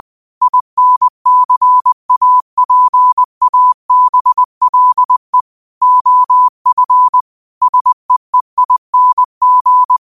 Morse code, heard as a keyed tone, about 20 wpm, 1000 Hz. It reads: INCAPABLE OF SEEING